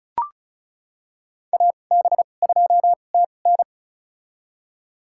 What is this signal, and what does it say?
Morse code, heard as a keyed tone, about 34 wpm, spaced as for 25 wpm, 700 Hz. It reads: AB2TN
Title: USA Call Signs - 5 Characters - 2x Character Spacing - 35wpm